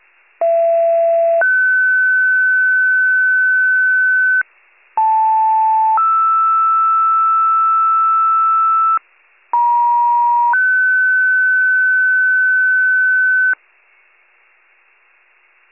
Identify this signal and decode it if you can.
Two-tone page, heard as a one-second tone, then a three-second tone, about 670 then 1500 Hz, about 880 then 1320 Hz, about 950 then 1500 Hz. It is heard from 0.4 to 13.5 s.
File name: pager / 75 FIRE / wcalert